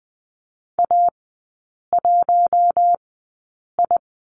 Morse code, heard as a keyed tone, about 20 wpm, spaced as for 8 wpm, 700 Hz. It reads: A1I
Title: 081 Lesson 14e - Number 2 - Three Characters 20wpm